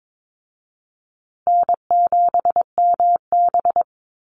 Morse code, heard as a keyed tone, about 22 wpm, 700 Hz. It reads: N7MB
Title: Easy Call Signs Repeated 3x - 22wpm